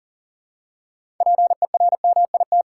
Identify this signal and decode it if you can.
Morse code, heard as a keyed tone, about 37 wpm, 700 Hz. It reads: PERMIT
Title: Six Letter Words - 40wpm